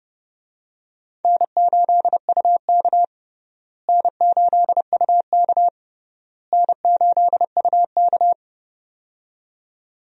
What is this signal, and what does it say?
Morse code, heard as a keyed tone, about 30 wpm, 700 Hz. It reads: N8UK N8UK N8UK
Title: USA Call Signs - 4 Characters - Repeated 3x - 30wpm